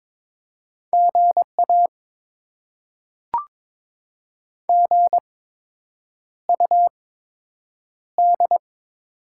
Morse code, heard as a keyed tone, about 22 wpm, spaced as for 8 wpm, 700 Hz. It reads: GA GUD
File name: Top 20 QSO Elements - 8x Character Spacing - 22wpm